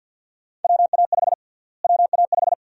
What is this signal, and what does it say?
Morse code, heard as a keyed tone, about 46 wpm, 700 Hz. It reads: WA5 WA5
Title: USA Call Sign Prefixes - Repeated 2x - 50wpm